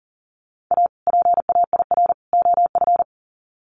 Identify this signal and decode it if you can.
Morse code, heard as a keyed tone, about 40 wpm, 700 Hz. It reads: A PAIR OF
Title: Sets of 3 Words 40wpm